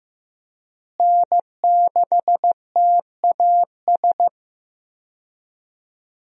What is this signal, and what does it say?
Morse code, heard as a keyed tone, about 15 wpm, 700 Hz. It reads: N6TAS
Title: USA Call Signs - 5 Characters - Repeated 2x - 15wpm